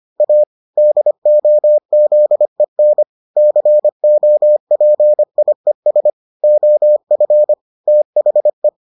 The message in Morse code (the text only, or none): A DOZEN COPIES OF THE